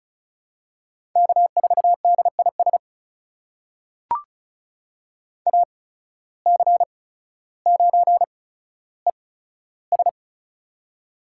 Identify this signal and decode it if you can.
Morse code, heard as a keyed tone, about 36 wpm, spaced as for 15 wpm, 700 Hz. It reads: K4DIS AC9ES
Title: USA Call Signs - 8x Character Spacing - 35wpm